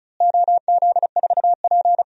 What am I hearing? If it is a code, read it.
Morse code, heard as a keyed tone, about 36 wpm, 700 Hz. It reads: OZ4P
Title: DXCC-100 Call Signs - 4 Characters - Repeated 2x - 35wpm